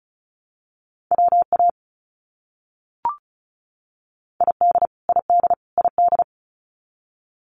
Morse code, heard as a keyed tone, about 34 wpm, 700 Hz. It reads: WA ID ID ID
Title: US States Repeated 3x 35wpm